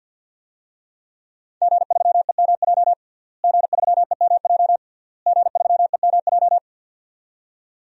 Morse code, heard as a keyed tone, about 51 wpm, 700 Hz. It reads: G3EMJ G3EMJ G3EMJ
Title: DXCC-100 Call Signs - 5 Characters - Repeated 3x - 50wpm